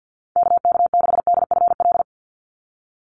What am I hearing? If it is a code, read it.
Morse code, heard as a keyed tone, about 50 wpm, 700 Hz. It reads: KK6DFL
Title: USA Call Signs 50wpm